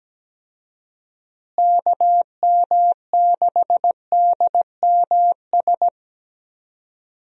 Morse code, heard as a keyed tone, about 17 wpm, 700 Hz. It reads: KM6DMS
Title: USA Call Signs - 6 Characters - Repeated 3x - 17wpm